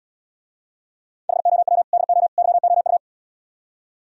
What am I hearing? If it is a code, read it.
Morse code, heard as a keyed tone, about 79 wpm, 700 Hz. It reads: HOW DO YOU